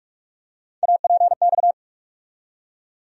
Morse code, heard as a keyed tone, about 46 wpm, 700 Hz. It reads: APX
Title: USA Call Sign Postfixes - 3 Letters 45wpm